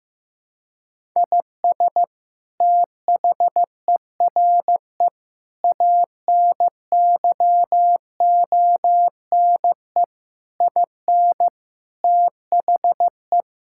Morse code, heard as a keyed tone, about 15 wpm, 700 Hz. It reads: IS THERE ANYONE IN THE
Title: Sentences from Top 1000 Words - Review Part 5 - 15wpm